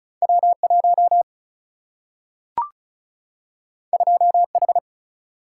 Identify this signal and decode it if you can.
Morse code, heard as a keyed tone, about 36 wpm, 700 Hz. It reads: W1 2H
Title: Single-Letter Single-Number 35wpm